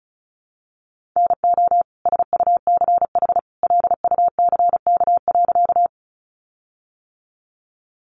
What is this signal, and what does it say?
Morse code, heard as a keyed tone, about 34 wpm, 700 Hz. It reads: NO SUCH LUCK.
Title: Sentences from Top 1500 Words 35wpm